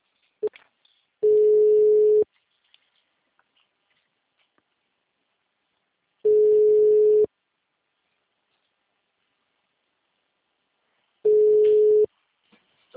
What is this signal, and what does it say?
Ringback tone (Europe, 425 Hz).